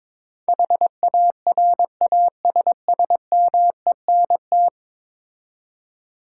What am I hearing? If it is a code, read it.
Morse code, heard as a keyed tone, about 22 wpm, 700 Hz. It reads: HARASSMENT
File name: practice / 22wpm / 10 Letter Words - First 3 Letters Given - 22wpm